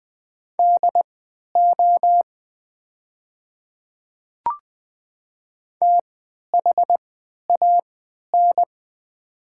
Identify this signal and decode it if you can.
Morse code, heard as a keyed tone, about 20 wpm, spaced as for 11 wpm, 700 Hz. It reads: DO THAN
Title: 3x Character Spacing - Top 100 Words - 20wpm